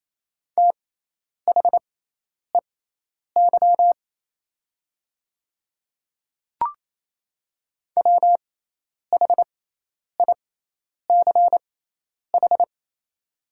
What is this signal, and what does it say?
Morse code, heard as a keyed tone, about 28 wpm, spaced as for 10 wpm, 700 Hz. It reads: THEY WHICH
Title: 6x Character Spacing - Top 200 Words - 28wpm